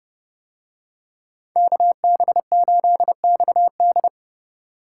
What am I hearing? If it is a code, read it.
Morse code, heard as a keyed tone, about 29 wpm, 700 Hz. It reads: KB8XD